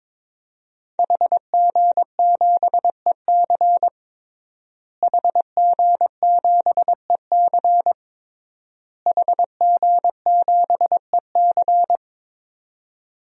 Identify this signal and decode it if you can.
Morse code, heard as a keyed tone, about 22 wpm, 700 Hz. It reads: HG7EC HG7EC HG7EC
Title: DXCC-100 Call Signs - 5 Characters - Repeated 3x - 22wpm